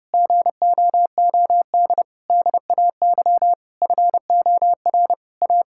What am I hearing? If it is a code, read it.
Morse code, heard as a keyed tone, about 32 wpm, 700 Hz. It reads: GOOD DAY FOR A
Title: Sentences from Top 1000 Words - Review Part 6 - 30wpm